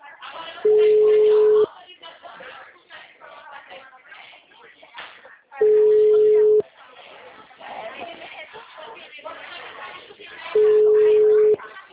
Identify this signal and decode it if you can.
Ringback tone (Europe, 425 Hz), with speech.